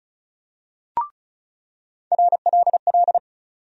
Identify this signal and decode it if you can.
Morse code, heard as a keyed tone, about 34 wpm, 700 Hz. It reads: RLL
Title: USA Call Sign Postfixes - 3 Letters 35wpm